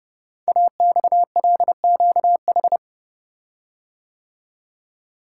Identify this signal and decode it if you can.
Morse code, heard as a keyed tone, about 30 wpm, 700 Hz. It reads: AXLQH
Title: Code Groups - 5 Letters - 30wpm